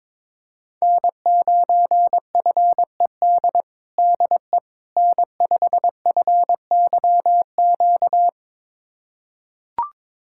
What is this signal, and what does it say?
Morse code, heard as a keyed tone, about 22 wpm, 700 Hz. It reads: N9FED DE N5FYQ
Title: USA Callsign DE Callsign - 5 Characters - 22wpm